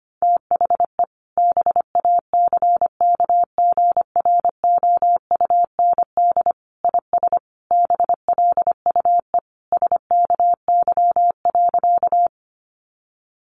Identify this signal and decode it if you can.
Morse code, heard as a keyed tone, about 25 wpm, 700 Hz. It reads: THE BACKGROUND IS BLUE SKY.
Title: Sentences from Top 1300 Words 25wpm